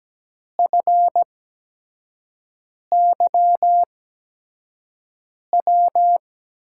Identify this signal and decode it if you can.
Morse code, heard as a keyed tone, about 17 wpm, spaced as for 5 wpm, 700 Hz. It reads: FYW
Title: USA Call Signs - 6 Characters - 8x Character Spacing - 17wpm